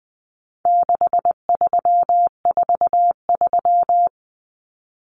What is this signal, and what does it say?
Morse code, heard as a keyed tone, about 20 wpm, 700 Hz. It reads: 6343
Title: Common Cut Numbers - 4-Digit 20wpm